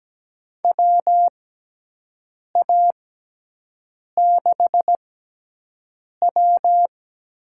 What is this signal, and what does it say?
Morse code, heard as a keyed tone, about 17 wpm, spaced as for 6 wpm, 700 Hz. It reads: WA6W
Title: USA Call Signs - 6 Characters - 6x Character Spacing - 17wpm